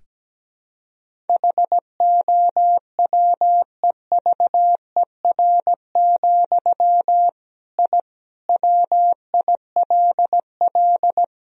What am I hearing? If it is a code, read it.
Morse code, heard as a keyed tone, about 17 wpm, 700 Hz. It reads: HOWEVER, I WILL
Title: Sentences from Top 1000 Words - Review Part 1 - 17wpm